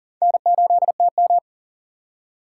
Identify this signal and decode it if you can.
Morse code, heard as a keyed tone, about 40 wpm, 700 Hz. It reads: N8TM
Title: USA Call Signs - 4 Characters - 3x Character Spacing - 40wpm